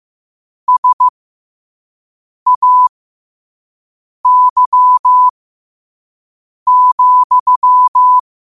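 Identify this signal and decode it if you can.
Morse code, heard as a keyed tone, about 15 wpm, spaced as for 5 wpm, 1000 Hz. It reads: SAY,
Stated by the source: Quote-of-the-day Morse code practice at 5 wpm.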